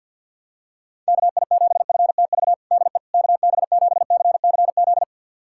Morse code, heard as a keyed tone, about 51 wpm, 700 Hz. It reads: KI7FTV DE KB7CCZ
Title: USA Callsign DE Callsign - 6 Characters - 50wpm